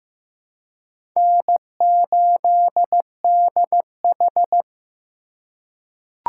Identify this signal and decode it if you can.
Morse code, heard as a keyed tone, about 15 wpm, 700 Hz. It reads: N8DH
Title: USA Call Signs - 4 Characters - 2x Character Spacing - 15wpm